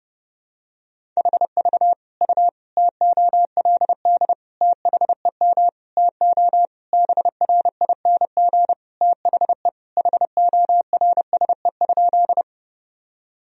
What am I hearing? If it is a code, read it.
Morse code, heard as a keyed tone, about 30 wpm, 700 Hz. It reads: HV U TOLD THEM TO BRING THE HORSE?
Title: Sentences from Top 500 Words and Abbreviations 30wpm